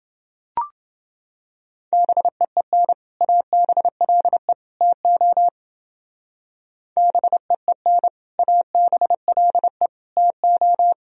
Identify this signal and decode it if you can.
Morse code, heard as a keyed tone, about 29 wpm, 700 Hz. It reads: BEEN ABLE TO BEEN ABLE TO